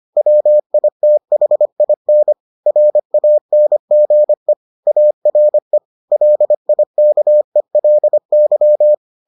Morse code, heard as a keyed tone, about 25 wpm, 600 Hz. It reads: WITHIN RANGE ARE LIKELY